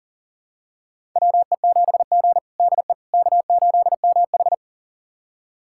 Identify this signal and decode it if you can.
Morse code, heard as a keyed tone, about 37 wpm, 700 Hz. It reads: WE7G DE K8MH
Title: USA Callsign DE Callsign - 4 Characters - 40wpm